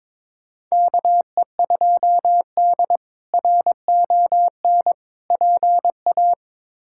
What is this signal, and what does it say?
Morse code, heard as a keyed tone, about 22 wpm, 700 Hz. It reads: KE2D RON PA
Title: K1USN SST QSO Practice - 22wpm